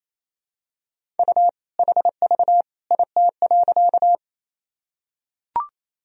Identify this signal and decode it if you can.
Morse code, heard as a keyed tone, about 28 wpm, 700 Hz. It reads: U HV IT.
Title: Sentences from Top 100 Words and Abbreviations 28wpm